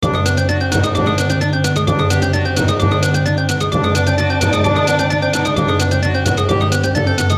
There is no code.